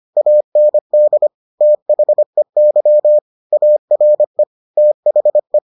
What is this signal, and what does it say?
Morse code, heard as a keyed tone, about 25 wpm, 600 Hz. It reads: AND THEY ARE THE